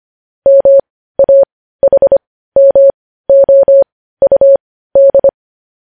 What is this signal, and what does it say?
Morse code, heard as a keyed tone, about 25 wpm, spaced as for 15 wpm, 550 Hz. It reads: MAHMOUD
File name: podcast / News headlines-1x-25x15